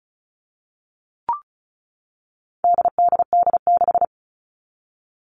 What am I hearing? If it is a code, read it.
Morse code, heard as a keyed tone, about 34 wpm, 700 Hz. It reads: DDD6
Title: All Cut Numbers - 4-Digit 35wpm